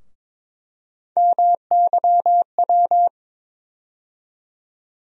Morse code, heard as a keyed tone, about 22 wpm, 700 Hz. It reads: MYW
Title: USA Call Sign Postfixes - 3 Letters 22wpm